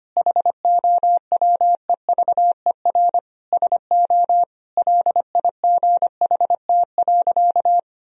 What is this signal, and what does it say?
Morse code, heard as a keyed tone, about 25 wpm, 700 Hz. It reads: HOWEVER SO LIGHT.
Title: Sentences from Top 1000 Words - Review Part 2 - 25wpm